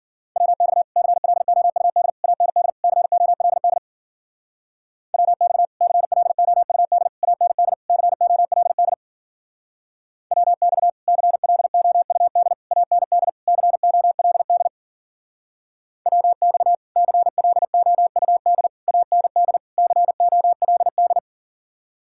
Morse code, heard as a keyed tone, about 50 wpm, 700 Hz. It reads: WX CLOUD AND COLD WX CLOUD AND COLD WX CLOUD AND COLD WX CLOUD AND COLD
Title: Speed-Racing Common QSO Phrases - 40wpm